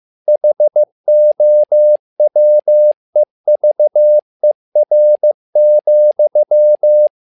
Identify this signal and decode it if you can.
Morse code, heard as a keyed tone, about 15 wpm, 600 Hz. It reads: HOWEVER,
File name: learning center / tower trouble / tower trouble 15wpm